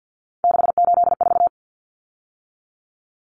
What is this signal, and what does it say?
Morse code, heard as a keyed tone, about 48 wpm, 700 Hz. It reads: 684
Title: Numbers - 3-Digit 50wpm